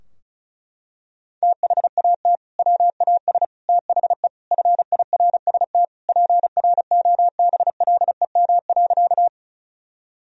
Morse code, heard as a keyed tone, about 34 wpm, 700 Hz. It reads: THAT WAS THE FIRST PROBLEM.
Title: Sentences from Top 1000 Words - Review Part 2 - 35wpm